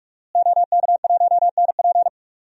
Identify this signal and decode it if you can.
Morse code, heard as a keyed tone, about 43 wpm, 700 Hz. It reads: OK1NP
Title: DXCC-100 Call Signs - 5 Characters - Repeated 3x - 45wpm